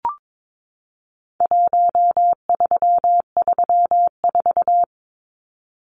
Morse code, heard as a keyed tone, about 22 wpm, 700 Hz. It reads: 1334
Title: Numbers - Contest Style - 4-Digit 22wpm